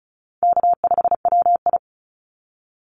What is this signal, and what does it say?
Morse code, heard as a keyed tone, about 34 wpm, 700 Hz. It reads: K5WI
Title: NA SOTA Chaser Callsigns 35wpm